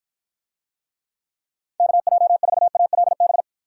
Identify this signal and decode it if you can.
Morse code, heard as a keyed tone, about 52 wpm, 700 Hz. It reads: KJ4ALB